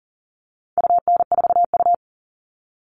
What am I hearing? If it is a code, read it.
Morse code, heard as a keyed tone, about 40 wpm, 700 Hz. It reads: UN4U
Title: All Cut Numbers - 4-Digit 40wpm